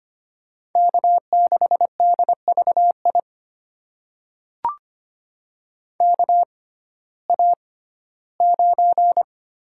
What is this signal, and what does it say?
Morse code, heard as a keyed tone, about 25 wpm, spaced as for 15 wpm, 700 Hz. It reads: K6DVI KA9
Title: USA Call Signs - 6x Character Spacing - 25wpm